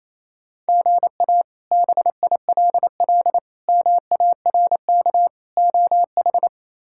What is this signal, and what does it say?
Morse code, heard as a keyed tone, about 28 wpm, 700 Hz. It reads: GA BILL MARK OH
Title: K1USN SST QSO Practice - 28wpm